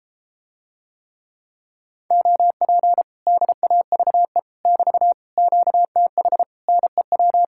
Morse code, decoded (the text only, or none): OP DAVE <BT> QTH NEW